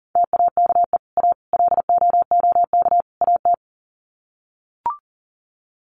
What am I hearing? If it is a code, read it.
Morse code, heard as a keyed tone, about 40 wpm, 700 Hz. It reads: TAKE A LOOK AT
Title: Sets of 4 Words 40wpm